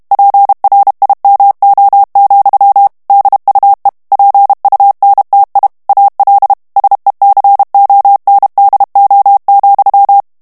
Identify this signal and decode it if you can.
Morse code, heard as a keyed tone, about 32 wpm, 800 Hz. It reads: PRIMO, DUE PUNTI AL SECONDO,
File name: ITALIEN texte clair 2 vitesse 30 mots minute